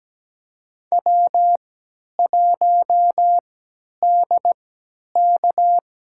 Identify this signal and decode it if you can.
Morse code, heard as a keyed tone, about 17 wpm, spaced as for 10 wpm, 700 Hz. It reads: W1DK
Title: USA Call Signs - 3x Character Spacing - 17wpm